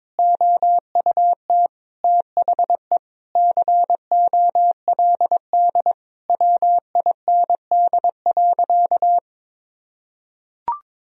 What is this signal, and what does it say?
Morse code, heard as a keyed tone, about 22 wpm, 700 Hz. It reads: OUT THE COLD WIND.
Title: Sentences from Top 1000 Words - Review Part 2 - 22wpm